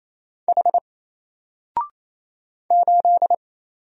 Morse code, heard as a keyed tone, about 27 wpm, 700 Hz. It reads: H 8
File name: Single-Letter-Number-28wpm